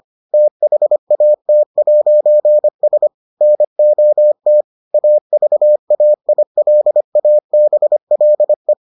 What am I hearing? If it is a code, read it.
Morse code, heard as a keyed tone, about 25 wpm, 600 Hz. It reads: THAT'S NOT AVAILABLE